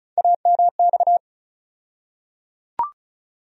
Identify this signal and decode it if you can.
Morse code, heard as a keyed tone, about 36 wpm, 700 Hz. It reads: AMX
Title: USA Call Sign Postfixes - 3 Letters 35wpm